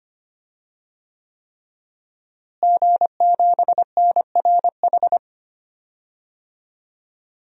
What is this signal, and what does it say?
Morse code, heard as a keyed tone, about 25 wpm, 700 Hz. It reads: G7NRH
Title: DXCC-100 Call Signs - 5 Characters - Repeated 3x - 25wpm